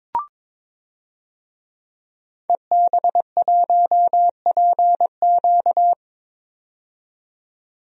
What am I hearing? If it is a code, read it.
Morse code, heard as a keyed tone, about 22 wpm, 700 Hz. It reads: EB1PQ
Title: DXCC-100 Call Signs - 5 Characters - 22wpm